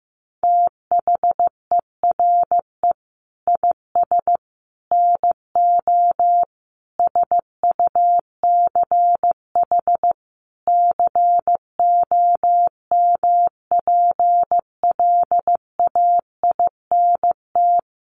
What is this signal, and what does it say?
Morse code, heard as a keyed tone, about 15 wpm, 700 Hz. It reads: THERE IS NO SUCH COMPLAINT
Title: Sentences from Top 1400 Words 15wpm